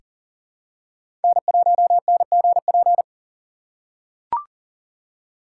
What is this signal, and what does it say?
Morse code, heard as a keyed tone, about 40 wpm, 700 Hz. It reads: N1NGP
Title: USA Call Signs - 5 Characters - 4x Character Spacing - 40wpm